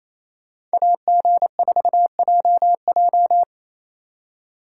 Morse code, heard as a keyed tone, about 36 wpm, 700 Hz. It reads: AG4JJ